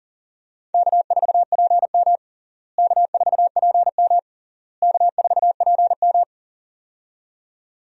Morse code, heard as a keyed tone, about 40 wpm, 700 Hz. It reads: K4PM K4PM K4PM
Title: USA Call Signs - 4 Characters - Repeated 3x - 40wpm